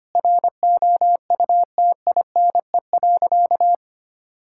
Morse code, heard as a keyed tone, about 25 wpm, 700 Hz. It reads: ROUTINE.
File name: practice / 25wpm / Difficult Sentences with Hints - 25wpm